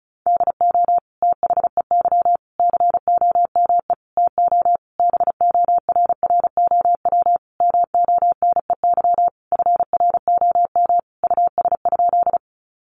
Morse code, heard as a keyed tone, about 34 wpm, 700 Hz. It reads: DO THEY COME TO BORROW MONEY FROM US?
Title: Sentences from Top 1500 Words 35wpm